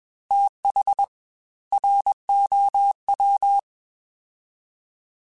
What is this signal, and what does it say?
Morse code, heard as a keyed tone, about 21 wpm, 800 Hz. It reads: TH ROW
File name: TH / TH mp3s / TH Prefixes spt @20